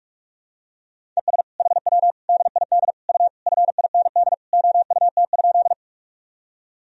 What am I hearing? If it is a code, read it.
Morse code, heard as a keyed tone, about 46 wpm, 700 Hz. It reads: ES HW DID U FIND OUT?
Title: Sentences from Top 100 Words and Abbreviations 45wpm